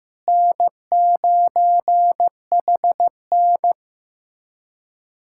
Morse code, heard as a keyed tone, about 15 wpm, 700 Hz. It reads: N9HN
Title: USA Callsign DE Callsign - 4 Characters - 15wpm